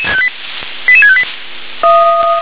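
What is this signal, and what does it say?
Five-tone selective call (ZVEI), heard as quick runs of five tones, tones of about 70 ms. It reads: 88658
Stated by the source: ZVEI five-tone call received on SDR.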